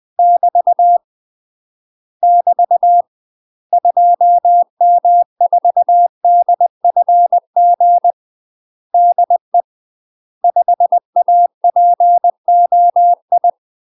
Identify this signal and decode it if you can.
Morse code, heard as a keyed tone, about 20 wpm, 700 Hz. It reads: <BT> <BT> 2M4DFG DE 5APOI
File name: UK MASTER QSOs TRAINING / 20WPM / UK MASTER QSOs 0070